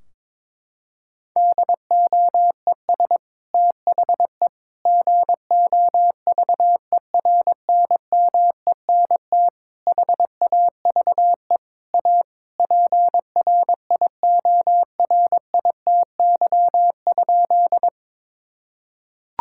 Morse code, heard as a keyed tone, about 22 wpm, 700 Hz. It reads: DOES THE GOVERNMENT HAVE A PRIORITY?
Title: Sentences from Top 1600 Words 22wpm